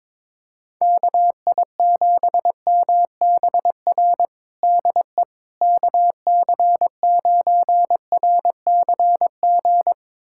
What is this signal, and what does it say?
Morse code, heard as a keyed tone, about 22 wpm, 700 Hz. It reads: KI7MBR DE KC9RCG
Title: USA Callsign DE Callsign - 6 Characters - 22wpm